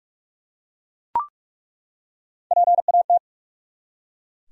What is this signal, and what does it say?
Morse code, heard as a keyed tone, about 46 wpm, 700 Hz. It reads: PAT
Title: USA Common Names - Max 5 Letters - 45wpm